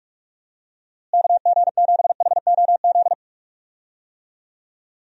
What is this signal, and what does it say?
Morse code, heard as a keyed tone, about 42 wpm, 700 Hz. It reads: KG7HOZ
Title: USA Call Signs - 6 Characters - Repeated 3x - 45wpm